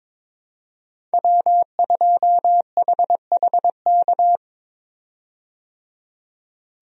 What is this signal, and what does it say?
Morse code, heard as a keyed tone, about 22 wpm, 700 Hz. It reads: W2HHK
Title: USA Call Signs - 5 Characters - 22wpm